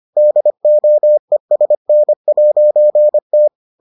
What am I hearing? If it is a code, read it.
Morse code, heard as a keyed tone, about 25 wpm, 600 Hz. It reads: DOESN'T